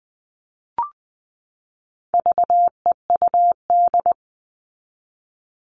Morse code, heard as a keyed tone, about 20 wpm, 700 Hz. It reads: VEUD
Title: All Cut Numbers - 4-Digit 20wpm